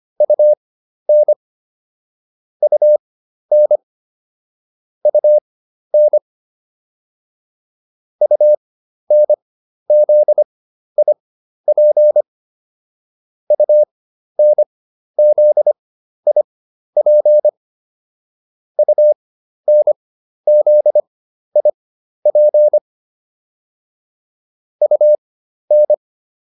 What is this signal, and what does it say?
Morse code, heard as a keyed tone, about 25 wpm, spaced as for 12 wpm, 600 Hz. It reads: UN UN UN UNZIP UNZIP UNZIP UN